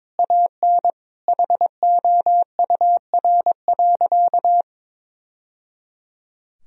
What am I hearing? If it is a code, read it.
Morse code, heard as a keyed tone, about 22 wpm, 700 Hz. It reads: AN HOUR.
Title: Sentences from Top 1000 Words - Review Part 4 - 22wpm